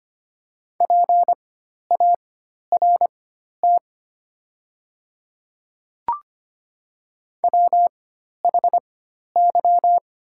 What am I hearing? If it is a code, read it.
Morse code, heard as a keyed tone, about 25 wpm, spaced as for 12 wpm, 700 Hz. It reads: PART WHY
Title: 4x Character Spacing - Top 200 Words - 25wpm